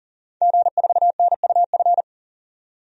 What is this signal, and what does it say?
Morse code, heard as a keyed tone, about 37 wpm, 700 Hz. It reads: G4NUF
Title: DXCC-100 Call Signs - 5 Characters - 40wpm